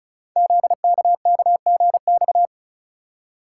Morse code, heard as a keyed tone, about 34 wpm, 700 Hz. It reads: ZKKGX
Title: Code Groups - 5 Letter:Numbers - 35wpm